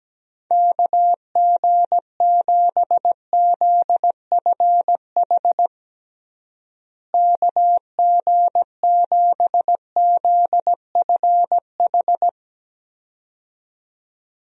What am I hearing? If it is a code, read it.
Morse code, heard as a keyed tone, about 17 wpm, 700 Hz. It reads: KG7ZFH KG7ZFH
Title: USA Call Signs - 6 Characters - Repeated 2x - 17wpm